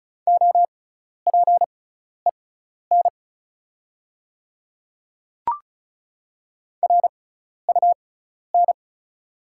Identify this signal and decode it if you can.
Morse code, heard as a keyed tone, about 36 wpm, spaced as for 12 wpm, 690 Hz. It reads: OPEN RUN
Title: 6x Character Spacing - Top 300 Words - 35wpm